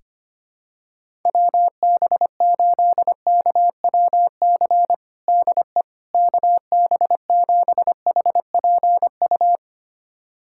Morse code, heard as a keyed tone, about 25 wpm, 700 Hz. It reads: WB8KWC DE KB7HPU
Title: USA Callsign DE Callsign - 6 Characters - 25wpm